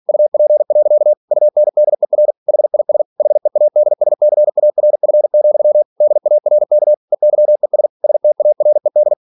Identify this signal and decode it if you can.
Morse code, heard as a keyed tone, about 46 wpm, 600 Hz. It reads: UP. UNDER HIS HEADSCARF, DARK EYES STARED
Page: Never 47wpm